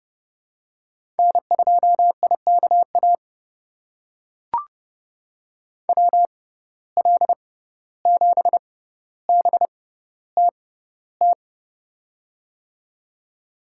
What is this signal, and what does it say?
Morse code, heard as a keyed tone, about 30 wpm, spaced as for 15 wpm, 700 Hz. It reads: N2IKA WL7BTT
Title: USA Call Signs - 6x Character Spacing - 30wpm